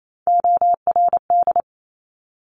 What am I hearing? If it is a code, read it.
Morse code, heard as a keyed tone, about 28 wpm, 700 Hz. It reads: ORD
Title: Top Three Letter Combinations in Words 28wpm